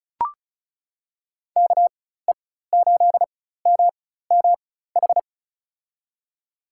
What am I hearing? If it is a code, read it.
Morse code, heard as a keyed tone, about 34 wpm, spaced as for 16 wpm, 700 Hz. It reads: KE8MMH
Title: USA Call Signs - 6 Characters - 4x Character Spacing - 35wpm